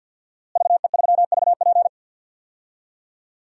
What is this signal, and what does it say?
Morse code, heard as a keyed tone, about 50 wpm, 700 Hz. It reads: VE3VP
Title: DXCC-100 Call Signs - 5 Characters - Repeated 2x - 50wpm